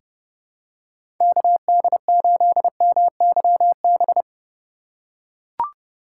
Morse code, heard as a keyed tone, about 30 wpm, 700 Hz. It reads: KD8MYB
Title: USA Call Signs - 6 Characters - 2x Character Spacing - 30wpm